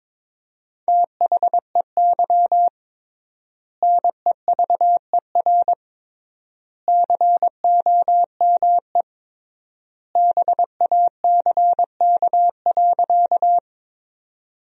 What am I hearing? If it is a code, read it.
Morse code, heard as a keyed tone, about 22 wpm, 700 Hz. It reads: THEY NEVER COME BACK.
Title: 3x Word-Spacing - Sentences from Top 300 Words - 22wpm